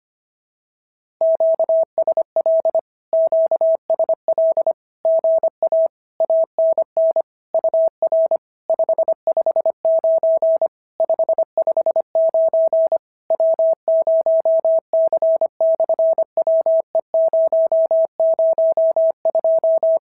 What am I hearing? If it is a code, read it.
Morse code, heard as a keyed tone, about 25 wpm, 650 Hz. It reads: QSL QSL GA ANN UR 559 559 W0C/WE002
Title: SOTA QSO Practice 25wpm